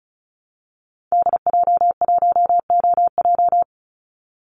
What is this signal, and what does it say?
Morse code, heard as a keyed tone, about 34 wpm, 700 Hz. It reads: DJ1OJ
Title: Call Signs Repeated 2x - 35wpm